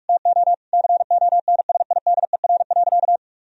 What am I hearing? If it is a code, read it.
Morse code, heard as a keyed tone, about 43 wpm, 700 Hz. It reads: TO CONSIDER.
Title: Sentences from Top 1000 Words - Review Part 4 - 45wpm